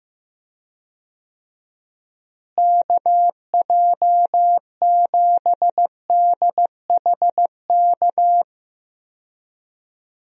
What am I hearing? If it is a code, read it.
Morse code, heard as a keyed tone, about 15 wpm, 700 Hz. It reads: KJ7DHK
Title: USA Call Signs - 6 Characters - Repeated 2x - 15wpm